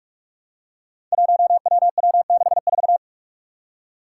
Morse code, heard as a keyed tone, about 44 wpm, 700 Hz. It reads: 1WW64
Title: Five Challenging Letter-Number Combinations 45wpm